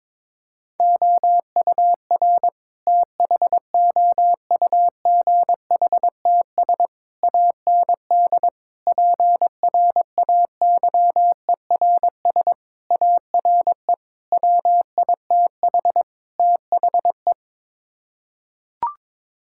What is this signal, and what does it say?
Morse code, heard as a keyed tone, about 21 wpm, 700 Hz. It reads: OUR THOUGHTS AND PRAYERS ARE WITH THE
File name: practice / 22wpm / Sets of 7 Words - 22wpm